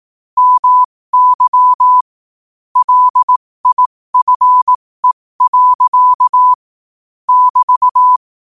Morse code, heard as a keyed tone, about 18 wpm, spaced as for 15 wpm, 1000 Hz. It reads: MY LIFE. <BT>
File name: qotdfiles / 15wpm